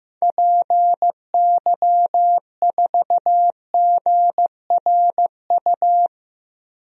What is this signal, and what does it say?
Morse code, heard as a keyed tone, about 15 wpm, 700 Hz. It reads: PY4GRU
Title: DXCC-100 Call Signs - 6 Characters - Repeated 2x - 15wpm